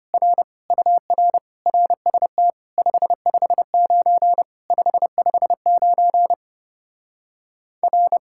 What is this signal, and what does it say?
Morse code, heard as a keyed tone, about 33 wpm, 700 Hz. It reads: R UR RST 559 559 R